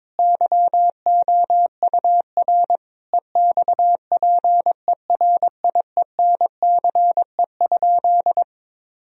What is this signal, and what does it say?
Morse code, heard as a keyed tone, about 22 wpm, 700 Hz. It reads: YOUR EXPERIENCE?
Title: Sentences from Top 1000 Words - Review Part 4 - 22wpm